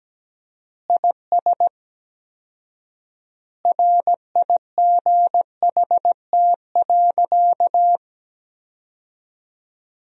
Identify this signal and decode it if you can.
Morse code, heard as a keyed tone, about 17 wpm, 700 Hz. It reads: IS RIGHT.
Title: 4x Word-Spacing - Sentences from Top 200 Words - 17wpm